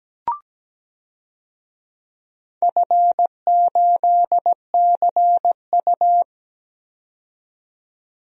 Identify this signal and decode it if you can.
Morse code, heard as a keyed tone, about 17 wpm, 700 Hz. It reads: F8CU
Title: DXCC-100 Call Signs - 4 Characters - 17wpm